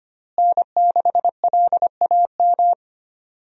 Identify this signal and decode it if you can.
Morse code, heard as a keyed tone, about 25 wpm, 700 Hz. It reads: N6LAM
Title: USA Call Signs - 5 Characters - Repeated 3x - 25wpm